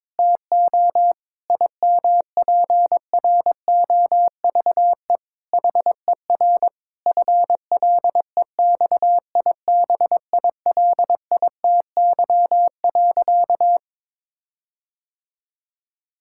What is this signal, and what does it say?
Morse code, heard as a keyed tone, about 22 wpm, 700 Hz. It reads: TO IMPROVE HER FLEXIBILITY.
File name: Difficult Sentences with Hints - 22wpm